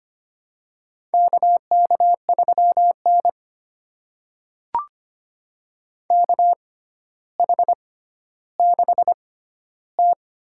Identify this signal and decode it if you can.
Morse code, heard as a keyed tone, about 25 wpm, spaced as for 13 wpm, 700 Hz. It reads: KK3N KH6T
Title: USA Call Signs - 4 Characters - 6x Character Spacing - 25wpm